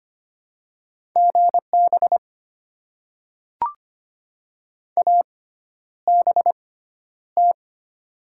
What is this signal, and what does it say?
Morse code, heard as a keyed tone, about 25 wpm, spaced as for 11 wpm, 700 Hz. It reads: GB ABT
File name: Top 90 QSO Elements - 6x Character Spacing - 25wpm